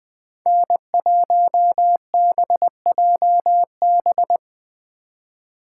Morse code, heard as a keyed tone, about 21 wpm, 700 Hz. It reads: N1BJB